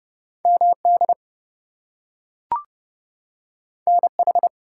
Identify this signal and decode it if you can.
Morse code, heard as a keyed tone, about 30 wpm, 700 Hz. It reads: MD NH
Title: US States and Canadian Provinces 30wpm